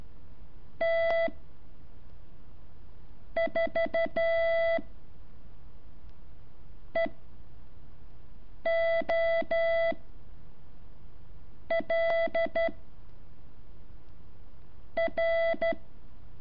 Morse code, heard as a keyed tone, about 12 wpm, 660 Hz. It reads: T 4 E O L R
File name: curso / ejercicio 8